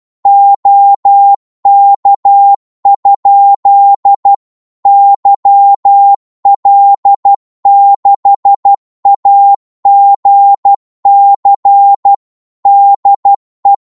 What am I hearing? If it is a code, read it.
Morse code, heard as a keyed tone, about 12 wpm, 800 Hz. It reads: OK? YL6AGC DE
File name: Morse / qso-12wpm